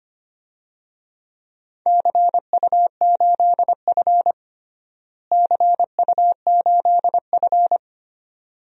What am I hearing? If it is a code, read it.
Morse code, heard as a keyed tone, about 25 wpm, 700 Hz. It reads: CU8F CU8F
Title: DXCC-100 Call Signs - 4 Characters - Repeated 2x - 25wpm